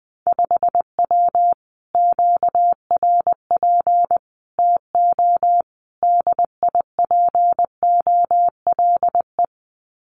Morse code, heard as a keyed tone, about 20 wpm, 700 Hz. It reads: 5W QRP TO DIPOLE
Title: Common QSO Phrases Encore 20wpm